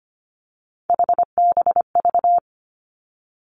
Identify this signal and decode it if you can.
Morse code, heard as a keyed tone, about 25 wpm, 700 Hz. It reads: HBV
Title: Three Challenging Letter-Number Combinations 25wpm